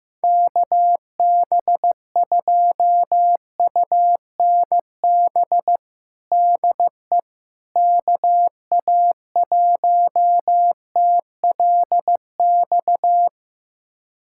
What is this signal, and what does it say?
Morse code, heard as a keyed tone, about 15 wpm, 700 Hz. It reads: KB2UNB DE KA1TLX
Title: USA Callsign DE Callsign - 6 Characters - 15wpm